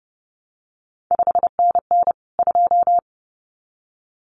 Morse code, heard as a keyed tone, about 30 wpm, 700 Hz. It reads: 5NN 2
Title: CQ WW DX Contest Exchange - 30wpm